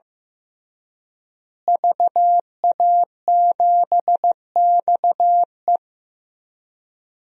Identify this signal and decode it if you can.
Morse code, heard as a keyed tone, about 15 wpm, 700 Hz. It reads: VA7XE
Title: DXCC-100 Call Signs - 5 Characters - Repeated 3x - 15wpm